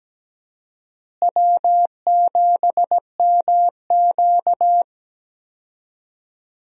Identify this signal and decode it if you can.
Morse code, heard as a keyed tone, about 17 wpm, 700 Hz. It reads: W7MQ
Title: USA Call Signs - 4 Characters - Repeated 3x - 17wpm